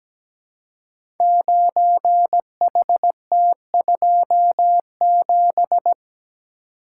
Morse code, heard as a keyed tone, about 17 wpm, 700 Hz. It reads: 9HT27
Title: Code Groups - 5 Letter:Numbers - 17wpm